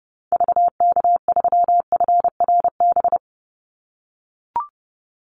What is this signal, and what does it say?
Morse code, heard as a keyed tone, about 30 wpm, 700 Hz. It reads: VK3FRB
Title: USA Call Signs 30wpm